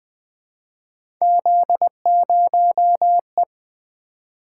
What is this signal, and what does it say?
Morse code, heard as a keyed tone, about 20 wpm, 700 Hz. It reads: Z0E
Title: Three Letter-Number Combinations - 20wpm